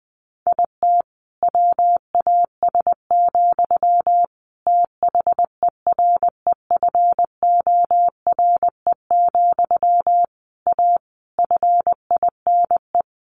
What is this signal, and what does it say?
Morse code, heard as a keyed tone, about 20 wpm, 700 Hz. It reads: IT WAS, THEREFORE, A FINE
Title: Sentences from Top 1400 Words 20wpm